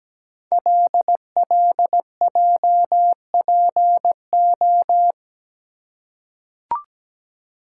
Morse code, heard as a keyed tone, about 17 wpm, 700 Hz. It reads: LLJPO
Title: Code Groups - 5 Letters - 17wpm